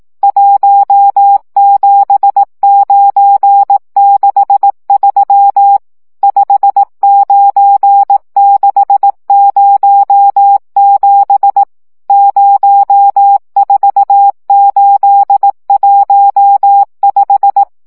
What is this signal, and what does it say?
Morse code, heard as a keyed tone, about 18 wpm, 800 Hz. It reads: 17963 59607 04815
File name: morze vizsga 2011-1 15